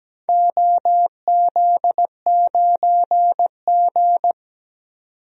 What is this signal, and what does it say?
Morse code, heard as a keyed tone, about 17 wpm, 700 Hz. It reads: OZ9G
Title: DXCC-100 Call Signs - 4 Characters - 17wpm